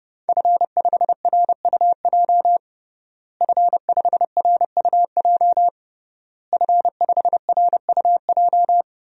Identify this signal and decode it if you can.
Morse code, heard as a keyed tone, about 30 wpm, 700 Hz. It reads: F5RUJ F5RUJ F5RUJ
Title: DXCC-100 Call Signs - 5 Characters - Repeated 3x - 30wpm